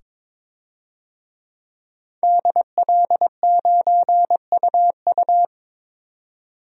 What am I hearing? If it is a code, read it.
Morse code, heard as a keyed tone, about 22 wpm, 700 Hz. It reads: DL9UU
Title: DXCC-100 Call Signs - 5 Characters - 22wpm